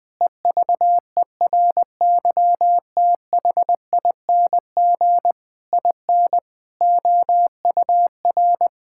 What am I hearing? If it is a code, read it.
Morse code, heard as a keyed tone, about 20 wpm, 700 Hz. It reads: EVERYTHING IN OUR
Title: Sentences with 8-Word Phrases - 20wpm